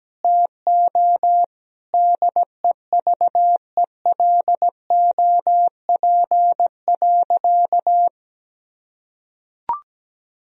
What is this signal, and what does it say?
Morse code, heard as a keyed tone, about 17 wpm, 700 Hz. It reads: TO DEVELOP.
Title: Sentences from Top 1000 Words - Review Part 2 - 17wpm